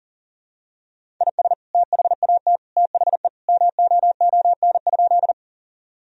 Morse code, heard as a keyed tone, about 40 wpm, 700 Hz. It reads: IS THAT THE MOON?
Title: Sentences from Top 1000 Words - Review Part 3 - 40wpm